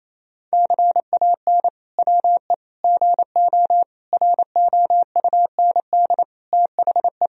Morse code, heard as a keyed tone, about 28 wpm, 700 Hz. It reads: CAN WE GO ROUND THE
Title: Sentences from Top 1000 Words - Review Part 1 - 28wpm